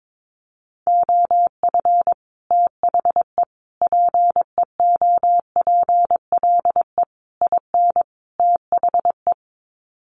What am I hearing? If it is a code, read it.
Morse code, heard as a keyed tone, about 22 wpm, 700 Hz. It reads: OF THE PEOPLE IN THE
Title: Sets of 5 Words Encore 22wpm